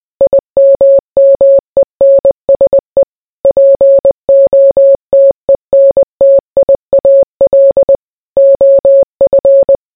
Morse code, heard as a keyed tone, about 20 wpm, 550 Hz. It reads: IMMENSE POTENTIAL OF